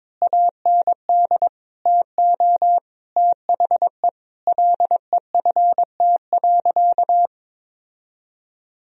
Morse code, heard as a keyed tone, about 22 wpm, 700 Hz. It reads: AND TO THE LEFT.
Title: Sentences from Top 1000 Words - Review Part 1 - 22wpm